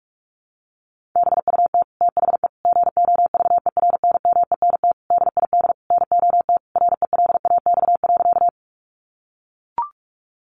Morse code, heard as a keyed tone, about 44 wpm, 700 Hz. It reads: BUT THE GOVERNMENT DID NOT RELAX.
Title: Sentences from Top 1400 Words 45wpm